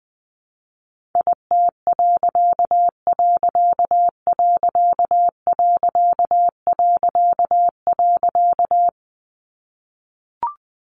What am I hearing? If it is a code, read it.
Morse code, heard as a keyed tone, about 20 wpm, 700 Hz. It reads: IT......
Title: A Sound Of Thunder 20wpm